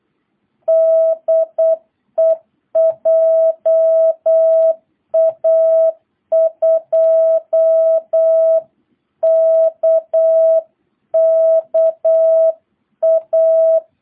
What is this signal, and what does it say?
Morse code, heard as a keyed tone, about 8 wpm, 660 Hz. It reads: DE JA2KKA